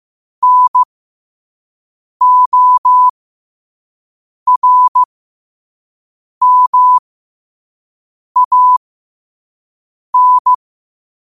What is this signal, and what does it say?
Morse code, heard as a keyed tone, about 15 wpm, spaced as for 5 wpm, 1000 Hz. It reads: NORMAN
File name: qotdfiles / 5wpm